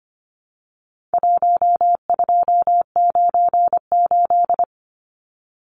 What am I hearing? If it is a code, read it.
Morse code, heard as a keyed tone, about 25 wpm, 700 Hz. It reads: 1298
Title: Numbers - 4-Digit 25wpm